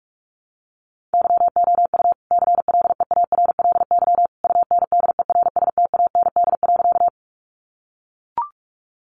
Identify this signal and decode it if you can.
Morse code, heard as a keyed tone, about 44 wpm, 700 Hz. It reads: YOU CLEARLY UNDERSTAND.
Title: Sentences from Top 1300 Words 45wpm